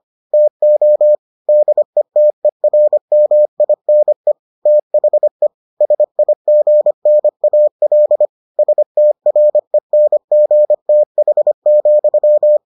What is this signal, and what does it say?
Morse code, heard as a keyed tone, about 25 wpm, 600 Hz. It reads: TO DETERMINE THE SIGNAL STRENGTH,